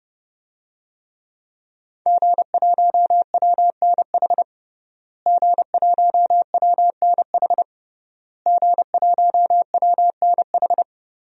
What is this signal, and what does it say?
Morse code, heard as a keyed tone, about 30 wpm, 700 Hz. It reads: G1WNH G1WNH G1WNH
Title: DXCC-100 Call Signs - 5 Characters - Repeated 3x - 30wpm